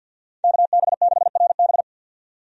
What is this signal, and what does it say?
Morse code, heard as a keyed tone, about 50 wpm, 700 Hz. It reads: KB6RB
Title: USA Call Signs - 5 Characters - 50wpm